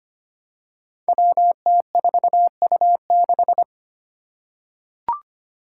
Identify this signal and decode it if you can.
Morse code, heard as a keyed tone, about 25 wpm, 700 Hz. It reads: WT4U6
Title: Five Challenging Letter-Number Combinations 25wpm